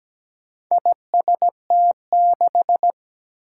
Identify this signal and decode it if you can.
Morse code, heard as a keyed tone, about 17 wpm, 700 Hz. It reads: IST6
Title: Four Challenging Letter-Number Combinations 17wpm